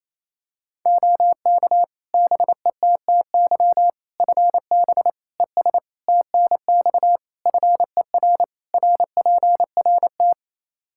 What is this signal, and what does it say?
Morse code, heard as a keyed tone, about 28 wpm, 700 Hz. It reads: OK BETTY FB ES TNX FER RPRT
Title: Practice QSO - Exchange 2 - Info - 28wpm